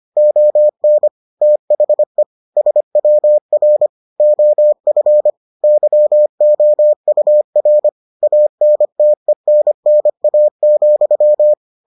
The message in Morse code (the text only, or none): ON THE SWR OF YOUR ANTENNA,